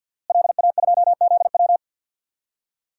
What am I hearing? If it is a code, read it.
Morse code, heard as a keyed tone, about 52 wpm, 700 Hz. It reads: LA2ZW
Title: DXCC-100 Call Signs - 5 Characters - Repeated 2x - 50wpm